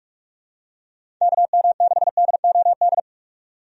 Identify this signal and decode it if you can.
Morse code, heard as a keyed tone, about 46 wpm, 700 Hz. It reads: KM6DOD
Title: USA Call Signs - 6 Characters - 45wpm